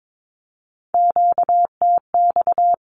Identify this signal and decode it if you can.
Morse code, heard as a keyed tone, about 22 wpm, 700 Hz. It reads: QTX
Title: Q-Codes 22wpm